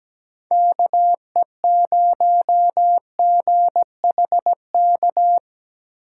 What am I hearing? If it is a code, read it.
Morse code, heard as a keyed tone, about 20 wpm, 700 Hz. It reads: KE0GHK
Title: Speed Racing - USA Call Signs 17wpm